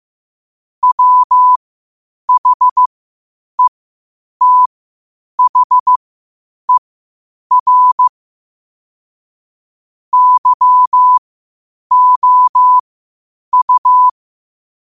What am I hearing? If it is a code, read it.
Morse code, heard as a keyed tone, about 15 wpm, spaced as for 8 wpm, 1000 Hz. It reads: WHETHER YOU
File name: qotdfiles / 8wpm